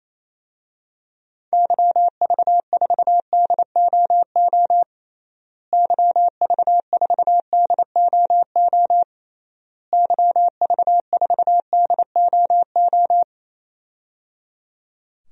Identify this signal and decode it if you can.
Morse code, heard as a keyed tone, about 28 wpm, 700 Hz. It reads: YV4DOO YV4DOO YV4DOO
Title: DXCC-100 Call Signs - 6 Characters - Repeated 3x - 28wpm